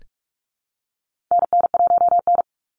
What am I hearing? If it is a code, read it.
Morse code, heard as a keyed tone, about 45 wpm, 700 Hz. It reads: NN1N
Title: Call Signs Repeated 3x - 45wpm